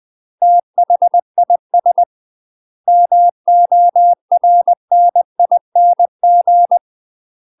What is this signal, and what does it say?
Morse code, heard as a keyed tone, about 20 wpm, 700 Hz. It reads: THIS MORNING
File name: UK MASTER QSOs TRAINING / 20WPM / UK MASTER QSOs 0036